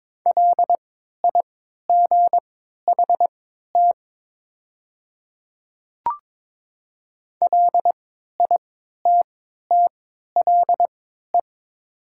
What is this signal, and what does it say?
Morse code, heard as a keyed tone, about 22 wpm, spaced as for 12 wpm, 700 Hz. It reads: LIGHT LITTLE
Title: 3x Character Spacing - Top 200 Words - 22wpm